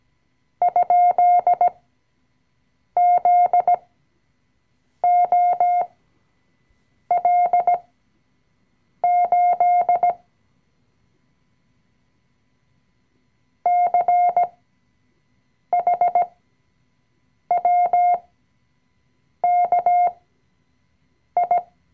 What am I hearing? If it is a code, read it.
Morse code, heard as a keyed tone, about 17 wpm, spaced as for 6 wpm, 710 Hz. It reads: ?ZOL8 CHWKI
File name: CW-Online-Kurs / Anfänger / Lektion 020